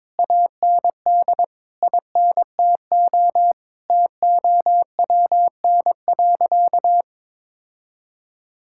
Morse code, heard as a keyed tone, about 22 wpm, 700 Hz. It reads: AND INTO TOWN.
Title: Sentences from Top 1000 Words - Review Part 6 - 22wpm